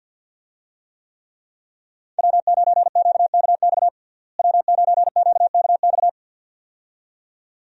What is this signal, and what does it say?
Morse code, heard as a keyed tone, about 46 wpm, 700 Hz. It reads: W9QKX W9QKX
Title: USA Call Signs - 5 Characters - Repeated 2x - 50wpm